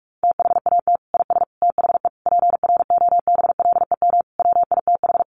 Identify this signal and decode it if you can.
Morse code, heard as a keyed tone, about 44 wpm, 700 Hz. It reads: THAT IS THE PROBLEM WITH
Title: Sentences from Top 1100 Words 45wpm